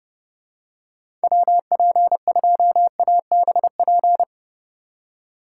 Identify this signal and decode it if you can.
Morse code, heard as a keyed tone, about 29 wpm, 700 Hz. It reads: WP2ABP
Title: USA Call Signs - 3x Character Spacing - 30wpm